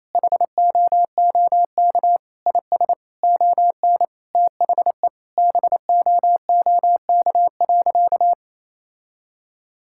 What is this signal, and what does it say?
Morse code, heard as a keyed tone, about 28 wpm, 700 Hz. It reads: HOOK IS ON THE BOOK.
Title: Sentences from Top 1600 Words 28wpm